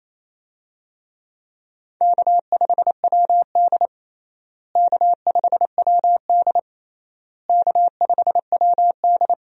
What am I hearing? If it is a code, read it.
Morse code, heard as a keyed tone, about 28 wpm, 700 Hz. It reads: K5WD K5WD K5WD
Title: USA Call Signs - 4 Characters - Repeated 3x - 28wpm